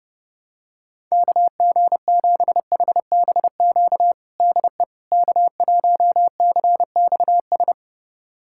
Morse code, heard as a keyed tone, about 30 wpm, 700 Hz. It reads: KG7HBQ DE K1CXS
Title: USA Callsign DE Callsign 30wpm